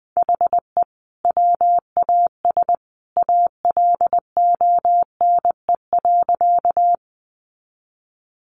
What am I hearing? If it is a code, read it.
Morse code, heard as a keyed tone, about 20 wpm, 700 Hz. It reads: HE WAS ALONE.
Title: Sentences from Top 1000 Words 20wpm